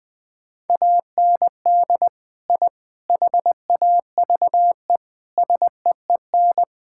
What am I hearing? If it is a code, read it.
Morse code, heard as a keyed tone, about 20 wpm, 700 Hz. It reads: AND I HAVE SEEN
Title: Sentences with 3-Word Phrases - 20wpm